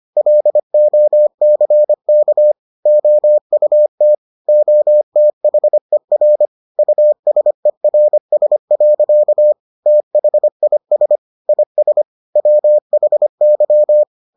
Morse code, heard as a keyed tone, about 25 wpm, 600 Hz. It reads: LOCK OUT OTHER USERS. THIS IS WHY